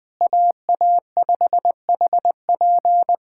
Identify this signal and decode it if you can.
Morse code, heard as a keyed tone, about 21 wpm, 700 Hz. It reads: AA5HP
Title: USA Callsign DE Callsign - 5 Characters - 20wpm